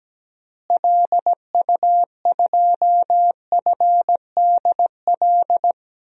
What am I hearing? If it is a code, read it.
Morse code, heard as a keyed tone, about 17 wpm, 700 Hz. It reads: LU2FDL
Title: DXCC-100 Call Signs - 6 Characters - Repeated 3x - 17wpm